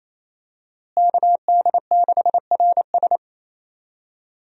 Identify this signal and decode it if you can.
Morse code, heard as a keyed tone, about 28 wpm, 700 Hz. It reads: KD6RS
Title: USA Call Signs - 5 Characters - Repeated 3x - 28wpm